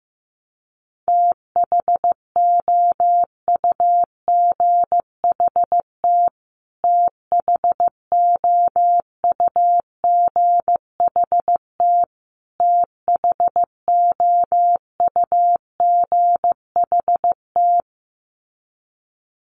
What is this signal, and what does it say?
Morse code, heard as a keyed tone, about 15 wpm, 700 Hz. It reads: THOUGHT THOUGHT THOUGHT
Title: Top 500 Words Repeated 15wpm